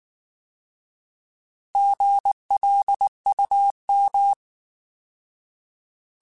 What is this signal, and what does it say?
Morse code, heard as a keyed tone, about 19 wpm, 800 Hz. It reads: GLUM